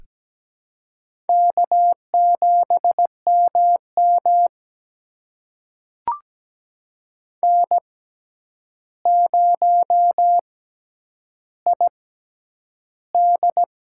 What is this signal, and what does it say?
Morse code, heard as a keyed tone, about 17 wpm, spaced as for 9 wpm, 700 Hz. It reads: K7MM N0ID
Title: USA Call Signs - 4 Characters - 6x Character Spacing - 17wpm